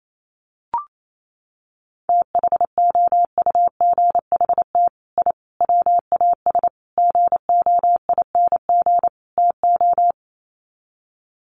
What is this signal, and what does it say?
Morse code, heard as a keyed tone, about 28 wpm, 700 Hz. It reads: THOUGHT I WAS GOING TO
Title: Sets of 5 Words Encore2 28wpm